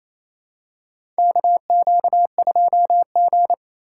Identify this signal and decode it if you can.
Morse code, heard as a keyed tone, about 28 wpm, 700 Hz. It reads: KQ2G
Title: USA Call Signs - 4 Characters - 3x Character Spacing - 28wpm